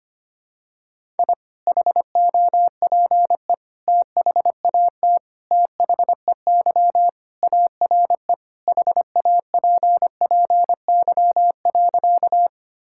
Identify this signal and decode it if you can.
Morse code, heard as a keyed tone, about 25 wpm, 700 Hz. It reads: I HOPE THAT THEY ARE HAPPY.
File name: practice / 25wpm / Sentences with 3-Word Phrases - 25wpm